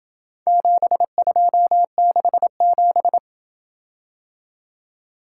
Morse code, heard as a keyed tone, about 27 wpm, 700 Hz. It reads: 7267